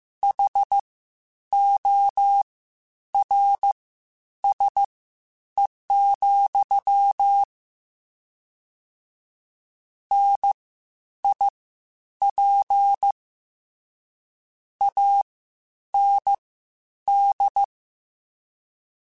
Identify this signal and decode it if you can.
Morse code, heard as a keyed tone, about 15 wpm, spaced as for 9 wpm, 800 Hz. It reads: HORSE, NIP AND
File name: Misc / Misc-mp3s / WordPairs sp @13